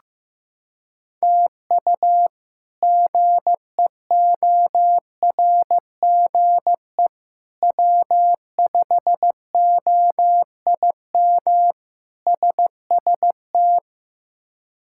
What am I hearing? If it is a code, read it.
Morse code, heard as a keyed tone, about 15 wpm, 700 Hz. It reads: TU GEORGE W5OIM SST
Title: K1USN SST QSO Practice - 15wpm